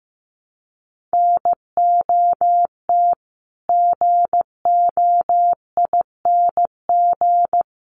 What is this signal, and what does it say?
Morse code, heard as a keyed tone, about 15 wpm, 700 Hz. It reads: NOT GOING
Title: Sets of 4 Words 15wpm